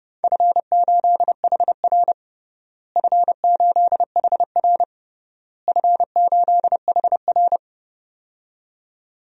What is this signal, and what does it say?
Morse code, heard as a keyed tone, about 30 wpm, 700 Hz. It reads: F8HR F8HR F8HR
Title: DXCC-100 Call Signs - 4 Characters - Repeated 3x - 30wpm